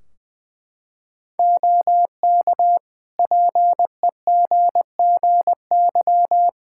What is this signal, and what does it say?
Morse code, heard as a keyed tone, about 20 wpm, 700 Hz. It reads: OK PEGGY
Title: Practice QSO - Exchange 3 - More Info - 20wpm